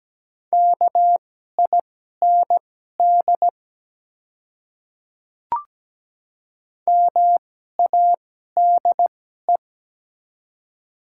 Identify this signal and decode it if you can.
Morse code, heard as a keyed tone, about 17 wpm, spaced as for 12 wpm, 700 Hz. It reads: KIND MADE
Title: 2x Character Spacing - Top 300 Words - 17wpm